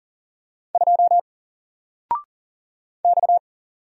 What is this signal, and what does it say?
Morse code, heard as a keyed tone, about 40 wpm, 700 Hz. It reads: 2 X
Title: Single Character (Letter:Number:Punctuation:Prosign) 40wpm